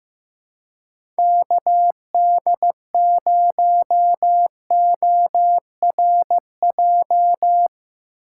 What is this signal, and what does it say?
Morse code, heard as a keyed tone, about 15 wpm, 700 Hz. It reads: KD0ORJ
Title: USA Call Signs - 6 Characters - 4x Character Spacing - 15wpm